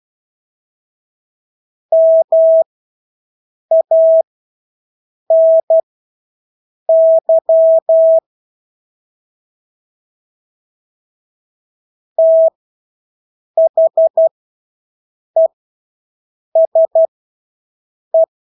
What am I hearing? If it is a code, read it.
Morse code, heard as a keyed tone, about 12 wpm, spaced as for 6 wpm, 650 Hz. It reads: MANY THESE
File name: Engelske ord 12-6 wpm